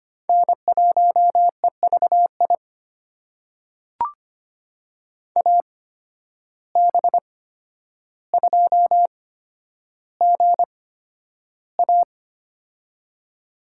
Morse code, heard as a keyed tone, about 25 wpm, spaced as for 11 wpm, 700 Hz. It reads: N1EVI AB2GA
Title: USA Call Signs - 5 Characters - 8x Character Spacing - 25wpm